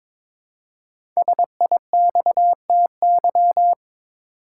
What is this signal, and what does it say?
Morse code, heard as a keyed tone, about 22 wpm, 700 Hz. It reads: SIXTY
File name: 5 Letter Words - First 3 Letters Given - 22wpm